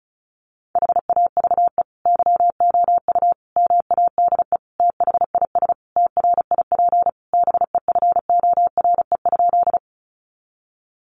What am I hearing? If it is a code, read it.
Morse code, heard as a keyed tone, about 34 wpm, 700 Hz. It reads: HAVE YOU MADE THIS TRIP BEFORE?
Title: Sentences from Top 1300 Words 35wpm